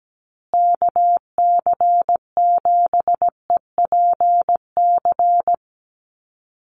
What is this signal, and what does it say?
Morse code, heard as a keyed tone, about 17 wpm, 700 Hz. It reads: KC7EPC
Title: USA Call Signs Repeated 3x 17wpm